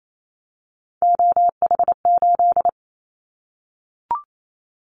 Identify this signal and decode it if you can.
Morse code, heard as a keyed tone, about 28 wpm, 700 Hz. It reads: OH8
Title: Callsign Prefixes for Top 25 DXCC - 28wpm